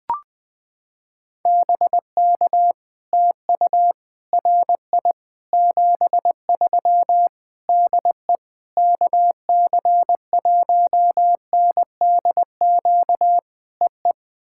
Morse code, heard as a keyed tone, about 20 wpm, 700 Hz. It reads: BK TU RI 73 DE KC1NDQ EE
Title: POTA QSO Practice - 20wpm